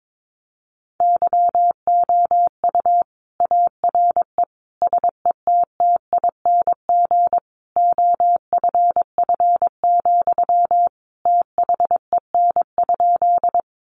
Morse code, heard as a keyed tone, about 22 wpm, 700 Hz. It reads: YOU ARE SETTING OFF, THEN?
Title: Sentences from Top 1300 Words 22wpm